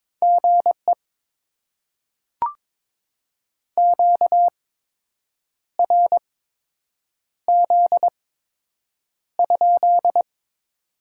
Morse code, heard as a keyed tone, about 22 wpm, spaced as for 7 wpm, 700 Hz. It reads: GE QRZ?
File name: Top 90 QSO Elements - 8x Character Spacing - 22wpm